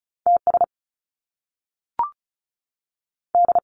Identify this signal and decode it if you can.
Morse code, heard as a keyed tone, about 36 wpm, 700 Hz. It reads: T S D
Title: Top Two Letter Combinations in Words 35wpm